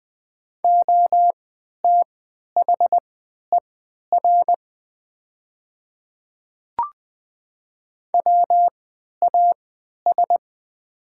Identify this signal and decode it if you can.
Morse code, heard as a keyed tone, about 20 wpm, spaced as for 11 wpm, 700 Hz. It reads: OTHER WAS
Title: 3x Character Spacing - Top 100 Words - 20wpm